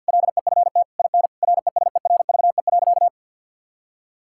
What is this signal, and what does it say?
Morse code, heard as a keyed tone, about 50 wpm, 700 Hz. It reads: LEFT IN RESERVE.
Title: Sentences from Top 1000 Words - Review Part 6 - 50wpm